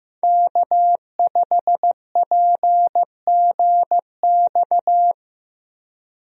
Morse code, heard as a keyed tone, about 15 wpm, 700 Hz. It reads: K5PGX
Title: USA Call Signs - 5 Characters - Repeated 2x - 15wpm